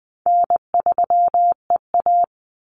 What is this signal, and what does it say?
Morse code, heard as a keyed tone, about 20 wpm, 700 Hz. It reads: N3EA
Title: Easy Call Signs - 20wpm